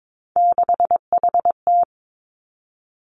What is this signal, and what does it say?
Morse code, heard as a keyed tone, about 22 wpm, 700 Hz. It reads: 6HT
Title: Three Challenging Letter-Number Combinations 22wpm